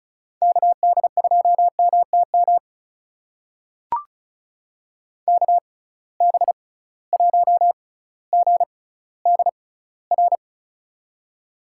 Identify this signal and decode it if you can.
Morse code, heard as a keyed tone, about 36 wpm, spaced as for 18 wpm, 700 Hz. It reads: KD2MTM KB1GDR
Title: USA Call Signs - 6 Characters - 6x Character Spacing - 35wpm